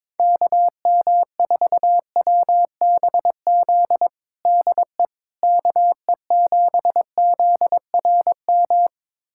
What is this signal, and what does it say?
Morse code, heard as a keyed tone, about 22 wpm, 700 Hz. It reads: KM4WBZ DE KE7ZRM
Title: USA Callsign DE Callsign 22wpm